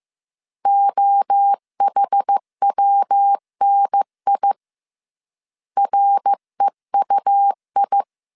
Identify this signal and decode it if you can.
Morse code, heard as a keyed tone, about 15 wpm, 800 Hz. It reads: OHWNI REUI